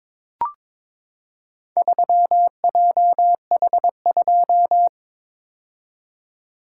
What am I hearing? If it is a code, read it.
Morse code, heard as a keyed tone, about 22 wpm, 700 Hz. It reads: 3JH2
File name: practice / 22wpm / Four Letter-Number Combinations - 22wpm